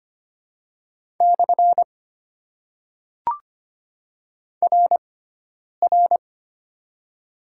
Morse code, heard as a keyed tone, about 25 wpm, 700 Hz. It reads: / R R
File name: Top 90 QSO Elements - 6x Character Spacing - 25wpm